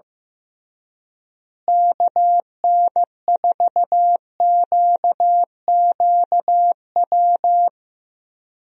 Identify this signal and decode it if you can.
Morse code, heard as a keyed tone, about 16 wpm, 700 Hz. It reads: KN4QQW
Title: USA Call Signs - 6 Characters - Repeated 2x - 15wpm